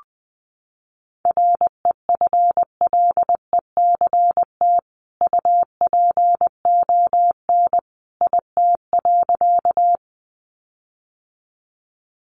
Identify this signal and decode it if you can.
Morse code, heard as a keyed tone, about 20 wpm, 700 Hz. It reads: REFLECT UPON IT.
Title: Sentences from Top 1500 Words 20wpm